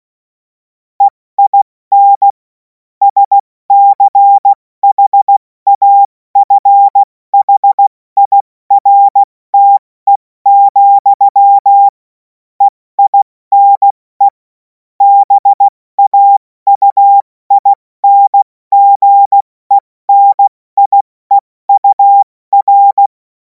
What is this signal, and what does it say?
Morse code, heard as a keyed tone, about 16 wpm, spaced as for 14 wpm, 800 Hz. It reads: EIN SCHAFHIRTE, EINE BAUINGENIEUR
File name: Morsekurs / Witze / Witz 14. April 2026 70 F80